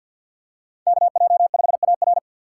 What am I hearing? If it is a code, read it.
Morse code, heard as a keyed tone, about 46 wpm, 700 Hz. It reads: KJ5AR
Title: USA Call Signs - 5 Characters - 6x Character Spacing - 50wpm